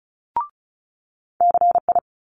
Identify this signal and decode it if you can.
Morse code, heard as a keyed tone, about 36 wpm, 700 Hz. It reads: CI
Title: Multi-letter Phonemes Quick Fire 35wpm